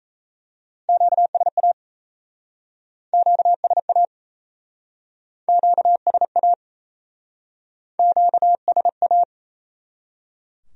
Morse code, heard as a keyed tone, about 37 wpm, 700 Hz. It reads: QSA QSA QSA QSA